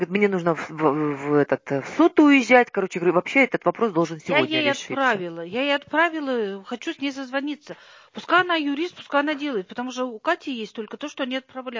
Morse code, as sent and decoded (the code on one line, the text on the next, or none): none